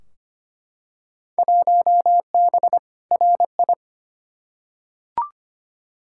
Morse code, heard as a keyed tone, about 25 wpm, 700 Hz. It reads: 1B RI
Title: Field Day Exchanges Encore - 25wpm